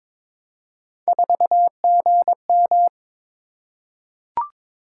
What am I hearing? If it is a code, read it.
Morse code, heard as a keyed tone, about 22 wpm, 700 Hz. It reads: 4GM
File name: practice / 22wpm / Three Letter-Number Combinations - 22wpm